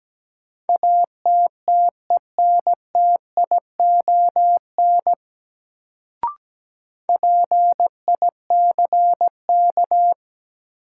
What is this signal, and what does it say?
Morse code, heard as a keyed tone, about 17 wpm, 700 Hz. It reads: ATTENTION PICK
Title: Sentences from Top 1000 Words - New Word Drill - 17wpm